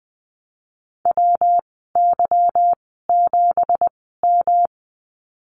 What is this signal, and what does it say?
Morse code, heard as a keyed tone, about 20 wpm, spaced as for 15 wpm, 700 Hz. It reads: WY7M
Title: 219 Lesson 35g - Number 0 - Call Signs 20wpm